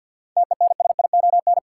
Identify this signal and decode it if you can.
Morse code, heard as a keyed tone, about 53 wpm, 700 Hz. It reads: TENSION
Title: Sentences from Top 1600 Words - New Word Drill Alt - 50wpm